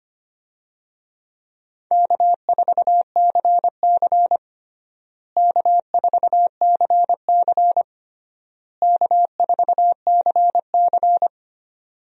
Morse code, heard as a keyed tone, about 25 wpm, 700 Hz. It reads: K4CC K4CC K4CC
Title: USA Call Signs - 4 Characters - Repeated 3x - 25wpm